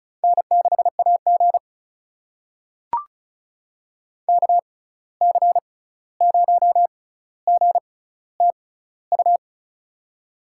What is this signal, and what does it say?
Morse code, heard as a keyed tone, about 34 wpm, spaced as for 16 wpm, 700 Hz. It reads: N6AG KC0GTU
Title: USA Call Signs - 6x Character Spacing - 35wpm